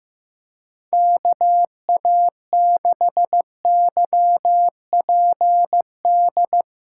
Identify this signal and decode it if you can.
Morse code, heard as a keyed tone, about 15 wpm, 700 Hz. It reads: KA6YPD
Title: USA Call Signs - 6 Characters - 3x Character Spacing - 15wpm